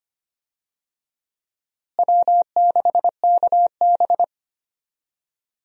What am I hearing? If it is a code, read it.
Morse code, heard as a keyed tone, about 25 wpm, 700 Hz. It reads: W6KB
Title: USA Call Signs - 4 Characters - Repeated 3x - 25wpm